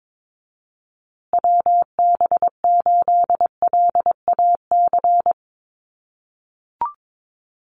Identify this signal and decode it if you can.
Morse code, heard as a keyed tone, about 22 wpm, 700 Hz. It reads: WB8LAC
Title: USA Call Signs Repeated 2x 22wpm